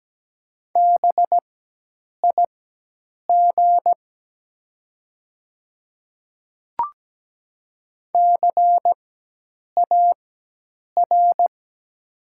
Morse code, heard as a keyed tone, about 17 wpm, spaced as for 8 wpm, 700 Hz. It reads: BIG CAR
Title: 4x Character Spacing - Top 300 Words - 17wpm